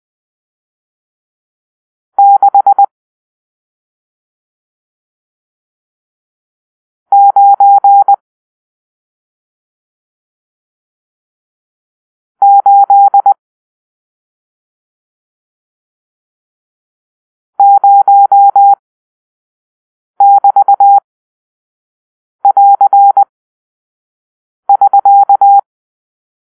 Morse code, heard as a keyed tone, about 20 wpm, spaced as for 3 wpm, 800 Hz. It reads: 6980<BT><AR><SK>
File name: arrl / Grp-7 Drill-1